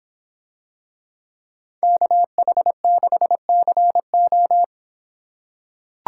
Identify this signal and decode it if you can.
Morse code, heard as a keyed tone, about 25 wpm, 700 Hz. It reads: KH6CO